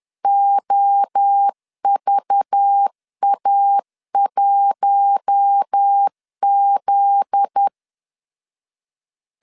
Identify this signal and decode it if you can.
Morse code, heard as a keyed tone, about 10 wpm, 800 Hz. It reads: OVA1Z
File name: LEZIONI / 8 YJ45 / 8 YJ 10wpm 2651